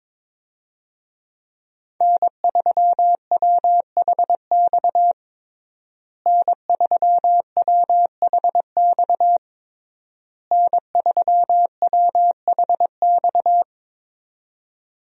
Morse code, heard as a keyed tone, about 22 wpm, 700 Hz. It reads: N3WHX N3WHX N3WHX
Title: USA Call Signs - 5 Characters - Repeated 3x - 22wpm